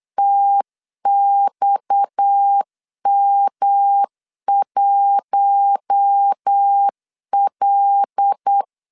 Morse code, heard as a keyed tone, about 8 wpm, 800 Hz. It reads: TXM1L
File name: LEZIONI / 7 GF9Z / 7 GF9Z 8wpm 3213